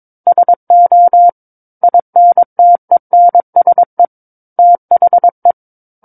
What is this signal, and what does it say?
Morse code, heard as a keyed tone, about 22 wpm, 700 Hz. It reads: SO INTENSE THE
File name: Book One / 13-Chapter 13 - 22 WPM